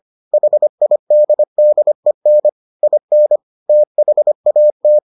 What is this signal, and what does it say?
Morse code, heard as a keyed tone, about 25 wpm, 600 Hz. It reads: HIDDEN IN THAT